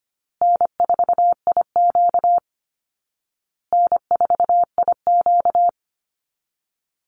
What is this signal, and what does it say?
Morse code, heard as a keyed tone, about 25 wpm, 700 Hz. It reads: N4IQ N4IQ
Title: Call Signs Repeated 2x - 25wpm